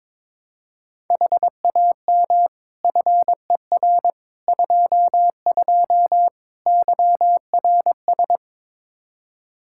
Morse code, heard as a keyed tone, about 21 wpm, 700 Hz. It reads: HAM FER 22 YRS
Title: K1USN SST QSO Practice 22wpm